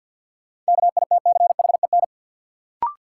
Morse code, heard as a keyed tone, about 52 wpm, 700 Hz. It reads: KITCHEN
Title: Sentences from Top 1400 Words - New Word Drill - 50wpm